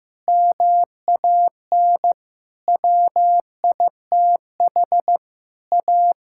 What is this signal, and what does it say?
Morse code, heard as a keyed tone, about 15 wpm, 700 Hz. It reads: MAN WITH A
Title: Sentences from Top 1000 Words - Review Part 6 - 15wpm